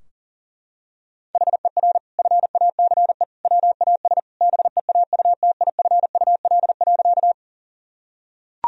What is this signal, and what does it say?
Morse code, heard as a keyed tone, about 40 wpm, 700 Hz. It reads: HER FACE WAS BEAUTIFUL.
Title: Sentences from Top 1000 Words - Review Part 2 - 40wpm